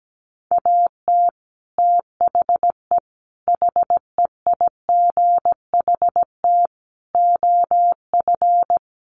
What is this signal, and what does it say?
Morse code, heard as a keyed tone, about 17 wpm, 700 Hz. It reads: AT THE HEIGHT OF
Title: Sets of 5 Words 17wpm